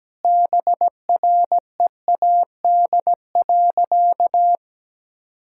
Morse code, heard as a keyed tone, about 17 wpm, 700 Hz. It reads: BREAD.
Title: Sentences from Top 1600 Words 17wpm